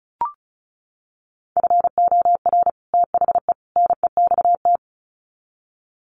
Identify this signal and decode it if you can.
Morse code, heard as a keyed tone, about 34 wpm, 700 Hz. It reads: FOR THE NEXT
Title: Sets of 3 Words 35wpm